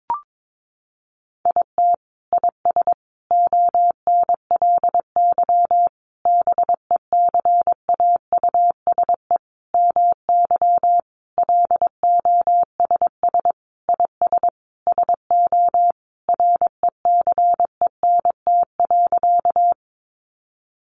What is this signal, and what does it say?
Morse code, heard as a keyed tone, about 22 wpm, 700 Hz. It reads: IT IS ONLY BECAUSE MY LOSS IS SO RECENT.
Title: Sentences from Top 1400 Words 22wpm